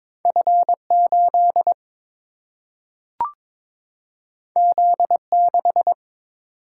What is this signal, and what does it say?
Morse code, heard as a keyed tone, about 22 wpm, 700 Hz. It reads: F 8 Z 6
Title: Single-Letter Single-Number 22wpm